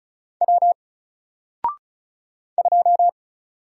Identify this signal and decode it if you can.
Morse code, heard as a keyed tone, about 36 wpm, 700 Hz. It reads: W 2
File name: Single-Letter-Number-35wpm